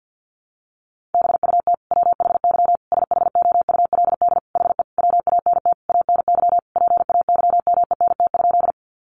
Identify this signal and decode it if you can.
Morse code, heard as a keyed tone, about 49 wpm, 700 Hz. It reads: BUT WHY SHOULD HE WANT ANY PAYMENT?
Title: Sentences from Top 1200 Words 50wpm